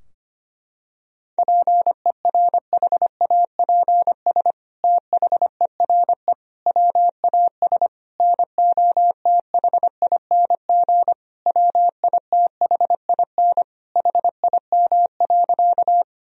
Morse code, heard as a keyed tone, about 25 wpm, 700 Hz. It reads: PERHAPS THERE WAS NOTHING WITHIN HIM.
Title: Sentences from Top 1000 Words - Review Part 4 - 25wpm